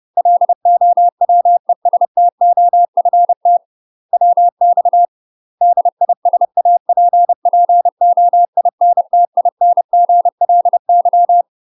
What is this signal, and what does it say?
Morse code, heard as a keyed tone, about 30 wpm, 700 Hz. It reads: LOWESTOFT WX DISAPPOINTINGLY
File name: UK MASTER QSOs TRAINING / 30WPM / UK MASTER QSOs 0034